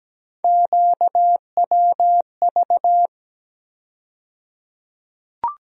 Morse code, heard as a keyed tone, about 17 wpm, 700 Hz. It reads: QWV
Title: USA Call Sign Postfixes - 3 Letters 17wpm